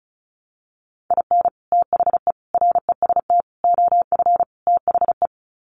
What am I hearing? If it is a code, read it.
Morse code, heard as a keyed tone, about 34 wpm, 700 Hz. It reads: IN THE REST OF THE
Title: Sets of 5 Words 35wpm